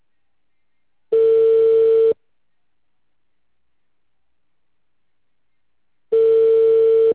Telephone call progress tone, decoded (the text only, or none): ringback tone